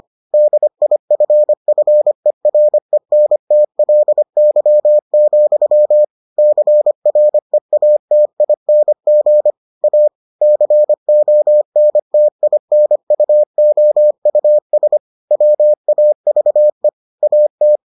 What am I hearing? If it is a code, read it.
Morse code, heard as a keyed tone, about 25 wpm, 600 Hz. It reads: DIFFERENTLY, CREATING A CONTINUOUS WAVE AT